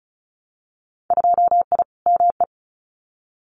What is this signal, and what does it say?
Morse code, heard as a keyed tone, about 36 wpm, 700 Hz. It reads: 2I ME
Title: Winter Field Day Exchanges Encore 35wpm